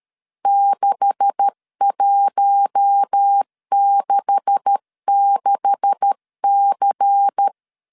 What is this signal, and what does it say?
Morse code, heard as a keyed tone, about 13 wpm, 800 Hz. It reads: 6166C
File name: LEZIONI / 9 286 12wpm 2305